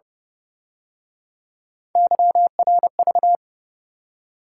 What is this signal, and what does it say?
Morse code, heard as a keyed tone, about 30 wpm, 700 Hz. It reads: YRV
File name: Speed-Racing - Three Letter-Number Combinations - 25wpm